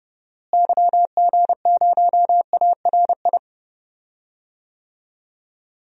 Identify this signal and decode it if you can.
Morse code, heard as a keyed tone, about 30 wpm, 700 Hz. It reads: YG0ARI
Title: DXCC-100 Call Signs - 6 Characters - Repeated 3x - 30wpm